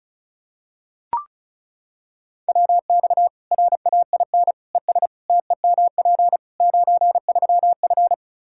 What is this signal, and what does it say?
Morse code, heard as a keyed tone, about 36 wpm, 700 Hz. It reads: WX RAIN ES TEMP 93F
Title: Practice QSO - Exchange 2 - Info - 35wpm